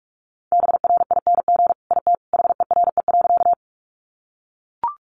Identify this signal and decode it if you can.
Morse code, heard as a keyed tone, about 44 wpm, 700 Hz. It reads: BRING IT HERE.
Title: Sentences from Top 500 Words 45wpm